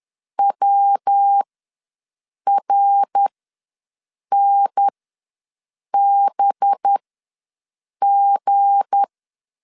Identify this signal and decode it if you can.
Morse code, heard as a keyed tone, about 10 wpm, spaced as for 6 wpm, 800 Hz. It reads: WRNBG
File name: LEZIONI / 10 SIGNS / 10C DOT ws 10wpm 4213